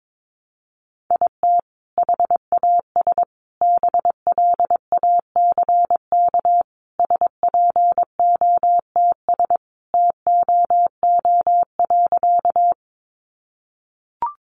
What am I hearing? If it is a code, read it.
Morse code, heard as a keyed tone, about 22 wpm, 700 Hz. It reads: IT HAS BLACK SPOTS TOO.
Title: Additional Easy Phrases 22wpm